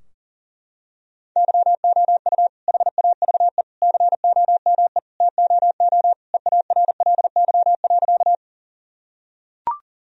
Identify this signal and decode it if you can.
Morse code, heard as a keyed tone, about 40 wpm, 700 Hz. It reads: YOU HAVE COME TOO EARLY.
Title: Sentences from Top 1000 Words - Review Part 2 - 40wpm